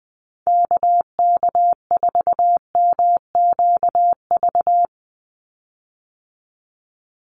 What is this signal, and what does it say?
Morse code, heard as a keyed tone, about 20 wpm, 700 Hz. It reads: KK4MQV
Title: USA Call Signs 20wpm